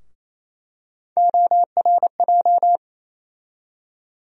Morse code, heard as a keyed tone, about 28 wpm, 700 Hz. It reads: ORJ
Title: USA Call Sign Postfixes - 3 Letters 28wpm